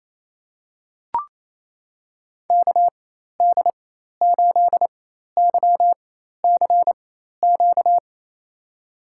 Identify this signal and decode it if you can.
Morse code, heard as a keyed tone, about 28 wpm, spaced as for 13 wpm, 700 Hz. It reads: KD8YCQ
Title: USA Call Signs - 6 Characters - 4x Character Spacing - 28wpm